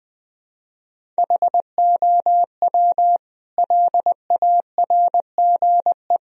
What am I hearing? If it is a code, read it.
Morse code, heard as a keyed tone, about 21 wpm, 700 Hz. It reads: HOW LARGE
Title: Sentences from Top 1000 Words - Review Part 1 - 20wpm